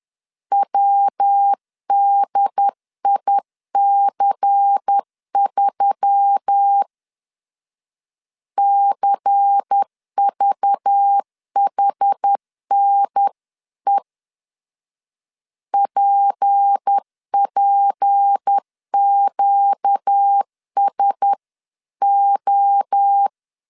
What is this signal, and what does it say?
Morse code, heard as a keyed tone, about 10 wpm, 800 Hz. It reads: WDIC3 CVHNE PPQSO
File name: LEZIONI / 6 PKX0 / 6 P 10wpm 2432